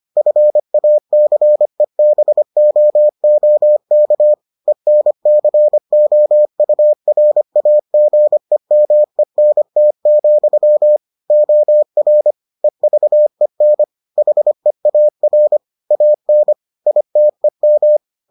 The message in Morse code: FACEBOOK ENCOURAGEMENT, OR EVEN HEAR AN ITEM

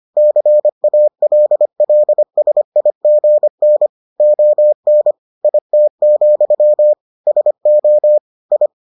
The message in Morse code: CALLSIGN ON IT, SO I

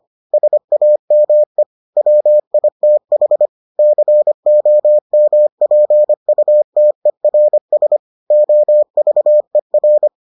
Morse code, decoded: SAME WITH COMPUTERS OVER

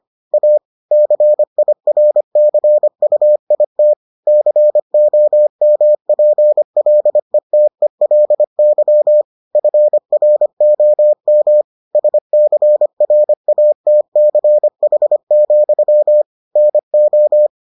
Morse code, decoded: A CIRCUIT COMPLETELY FROM SCRATCH, NO

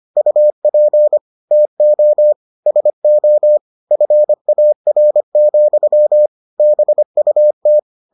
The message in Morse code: UP TO SO FAR, BUT

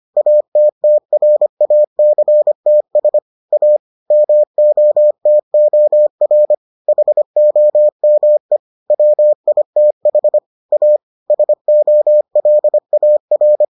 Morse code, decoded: ATTRACTS A MOTOR HOME WITH A SOLAR

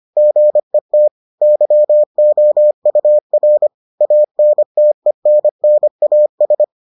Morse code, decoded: GET YOUR ANTENNAS

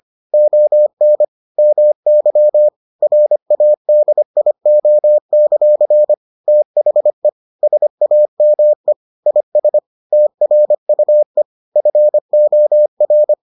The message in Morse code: ON MY RADIO; THE SAME IS TRUE FOR

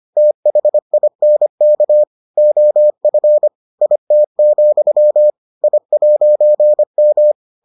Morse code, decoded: THINK OF IT, I'M

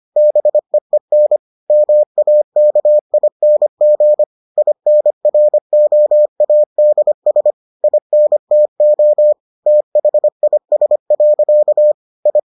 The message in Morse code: BEEN MAKING INROADS INTO THIS. I

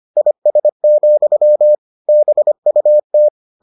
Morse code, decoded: IS, BUT

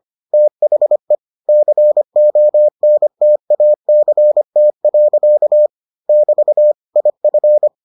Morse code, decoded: THE CONTACT. <BT> IF

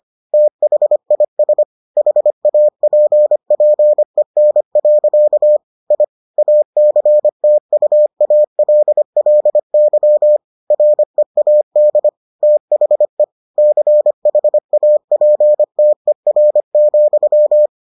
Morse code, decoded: THIS HAPPEN. I ACTUALLY READ THE CHAPTER,

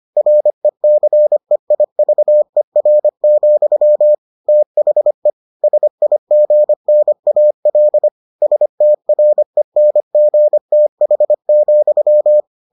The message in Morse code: RECEIVER, THE SIGNAL STRENGTH,

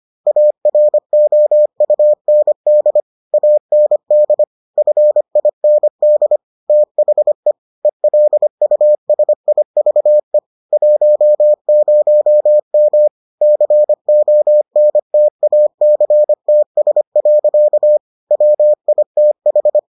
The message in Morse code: AROUND AND FIND THE ELUSIVE 10M CONTACTS. WITH

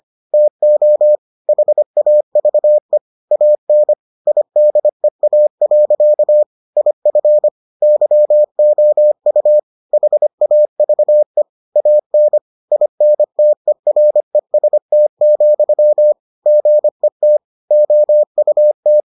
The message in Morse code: TO HAVE AN IDEA. IF YOU HAVE AN INTEREST, GET OUT

